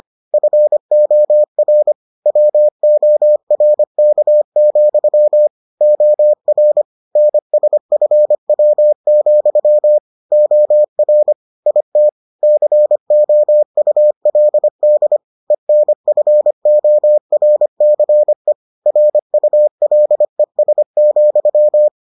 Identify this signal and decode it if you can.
Morse code, heard as a keyed tone, about 25 wpm, 600 Hz. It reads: FOR WORK, OR NSFW, OR IT COULD ENFORCE RULES,